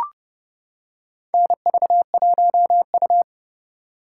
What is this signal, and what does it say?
Morse code, heard as a keyed tone, about 31 wpm, 700 Hz. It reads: NV1U
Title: K1USN SST QSO Practice - 30wpm